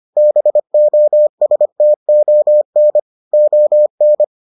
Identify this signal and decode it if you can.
Morse code, heard as a keyed tone, about 25 wpm, 600 Hz. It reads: BOSTON ON